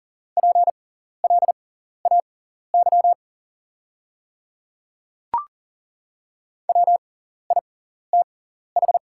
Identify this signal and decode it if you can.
Morse code, heard as a keyed tone, about 40 wpm, spaced as for 14 wpm, 690 Hz. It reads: PLAY WITH
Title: 6x Character Spacing - Top 300 Words - 40wpm